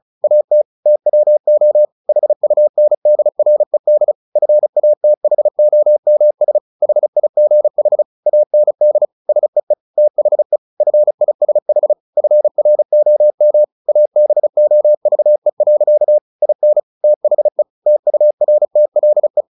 Morse code, heard as a keyed tone, about 34 wpm, 600 Hz. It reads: AT TWO HUNDRED FATHOMS HIGH AND SEE THE FISH FROM ABOVE. IN THE TURTLE
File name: the old man and the sea 2 35wpm